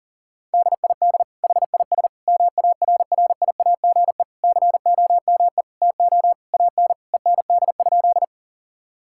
Morse code, heard as a keyed tone, about 40 wpm, 700 Hz. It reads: DID HIS MARRIAGE COME TO AN END?
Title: Sentences from Top 1000 Words - Review Part 5 - 40wpm